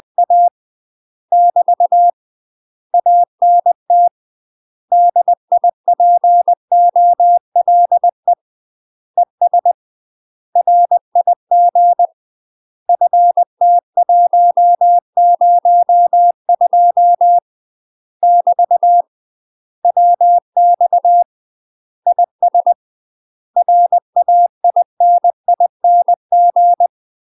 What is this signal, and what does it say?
Morse code, heard as a keyed tone, about 20 wpm, 700 Hz. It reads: A <BT> ANT DIPOLE ES RIG FT102 <BT> WX IS RAINING
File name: UK MASTER QSOs TRAINING / 20WPM / UK MASTER QSOs 0045